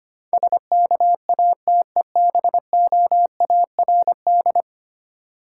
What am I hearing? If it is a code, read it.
Morse code, heard as a keyed tone, about 25 wpm, 690 Hz. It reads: SKATEBOARD
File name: practice / 25wpm / Easy Words with Hints - 25wpm